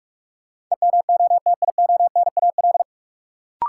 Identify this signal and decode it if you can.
Morse code, heard as a keyed tone, about 46 wpm, 700 Hz. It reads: EMOTIONAL
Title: Sentences from Top 1000 Words - Review Part 5 - New Word Drill - 45wpm